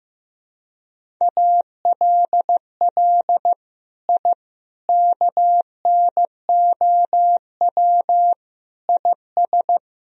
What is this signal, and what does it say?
Morse code, heard as a keyed tone, about 16 wpm, 700 Hz. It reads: ALL I KNOW IS
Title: Sentences from Top 1000 Words - Review Part 1 - 15wpm